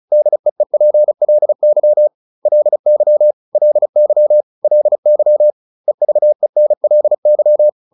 Morse code, heard as a keyed tone, about 34 wpm, 600 Hz. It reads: DEEPLY LY LY LY EVENLY